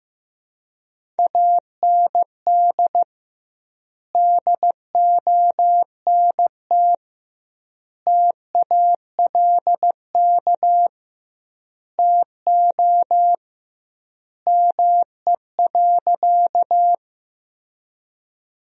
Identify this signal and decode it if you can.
Morse code, heard as a keyed tone, about 15 wpm, 700 Hz. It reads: AND DONT TALK TO ME.
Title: 2x Word-Spacing - Sentences from Top 400 Words - 15wpm